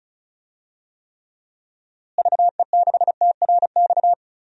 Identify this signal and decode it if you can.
Morse code, heard as a keyed tone, about 36 wpm, 700 Hz. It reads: VE6TRX
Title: DXCC-100 Call Signs - 6 Characters - Repeated 2x - 35wpm